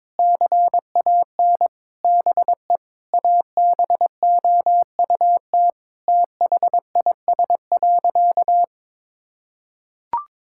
Morse code, heard as a keyed tone, about 22 wpm, 700 Hz. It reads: CAN BE ABOUT THIS.
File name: Sentences with 9-Word Phrases - 22wpm